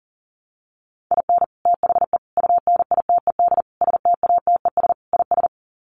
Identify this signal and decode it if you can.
Morse code, heard as a keyed tone, about 40 wpm, 700 Hz. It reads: IN THE UNITED STATES IS
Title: Sets of 5 Words 40wpm